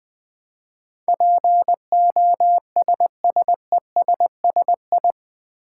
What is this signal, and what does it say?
Morse code, heard as a keyed tone, about 20 wpm, 700 Hz. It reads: POSSESSI
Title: Build A Word - 4 Letters At A Time - 10-Letter Words - 20wpm